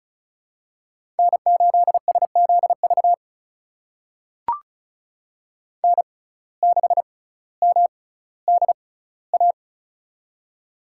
Morse code, heard as a keyed tone, about 36 wpm, spaced as for 18 wpm, 700 Hz. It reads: N8SZV N6MDA
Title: USA Call Signs - 6x Character Spacing - 35wpm